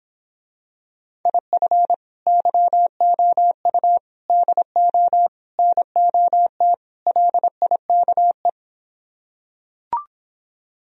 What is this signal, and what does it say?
Morse code, heard as a keyed tone, about 26 wpm, 700 Hz. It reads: IF YOU DO NOT LIKE